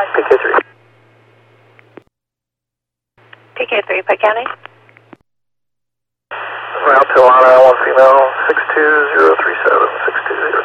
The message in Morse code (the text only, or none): none